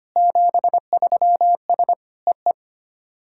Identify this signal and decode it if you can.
Morse code, heard as a keyed tone, about 25 wpm, 690 Hz. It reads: 73S EE
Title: SOTA QSO Practice 25wpm